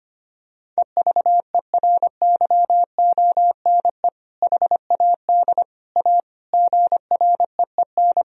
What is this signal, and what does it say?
Morse code, heard as a keyed tone, about 25 wpm, 700 Hz. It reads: EVERYONE HAD A GREEN
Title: Sentences from Top 1000 Words - Review Part 2 - 25wpm